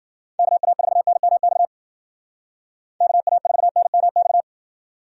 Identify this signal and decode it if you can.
Morse code, heard as a keyed tone, about 52 wpm, 700 Hz. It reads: KA4NMX KA4NMX